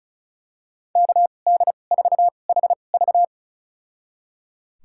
Morse code, heard as a keyed tone, about 36 wpm, spaced as for 25 wpm, 700 Hz. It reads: KD4HV
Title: USA Call Signs - 5 Characters - 2x Character Spacing - 35wpm